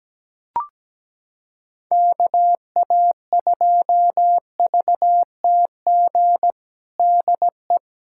Morse code, heard as a keyed tone, about 17 wpm, 700 Hz. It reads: KA2VTG DE
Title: USA Callsign DE Callsign - 6 Characters - 17wpm